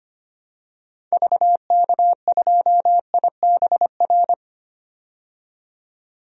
Morse code, heard as a keyed tone, about 25 wpm, 700 Hz. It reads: VK2IBR
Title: DXCC-100 Call Signs - 6 Characters - 25wpm